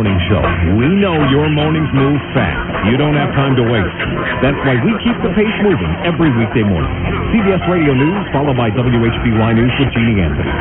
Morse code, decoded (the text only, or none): none